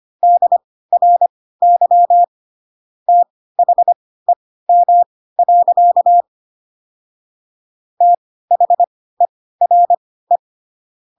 Morse code, heard as a keyed tone, about 25 wpm, spaced as for 16 wpm, 700 Hz. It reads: DRY THEM. THERE
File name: Pinocchio / Pinocchio ch6 10 25 ik1hge